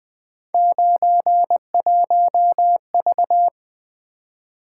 Morse code, heard as a keyed tone, about 23 wpm, 700 Hz. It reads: 91V
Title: Speed-Racing - Three Letter-Number Combinations - 20wpm